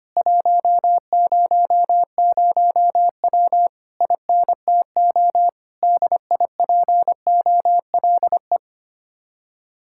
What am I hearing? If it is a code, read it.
Morse code, heard as a keyed tone, about 24 wpm, 700 Hz. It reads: 100W INTO DIPOLE
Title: Speed-Racing Common QSO Phrases Encore - 25wpm